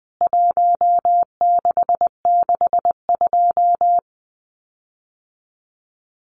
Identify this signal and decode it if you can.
Morse code, heard as a keyed tone, about 20 wpm, 700 Hz. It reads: 1662
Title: CWT Exchange 20wpm